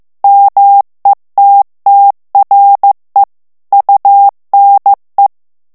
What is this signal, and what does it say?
Morse code, heard as a keyed tone, about 15 wpm, 800 Hz. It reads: METTRE UNE
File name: FRANCAIS TEXTE CLAIR 2 VITESSE 14 MOTS MINUTES